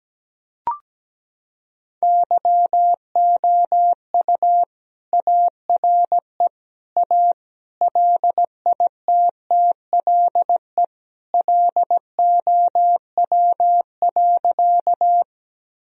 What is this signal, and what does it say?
Morse code, heard as a keyed tone, about 17 wpm, 700 Hz. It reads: YOU ARE A LITTLE LOW.
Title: Sentences from Top 1000 Words - Review Part 1 - 17wpm